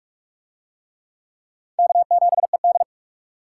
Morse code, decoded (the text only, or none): K7ED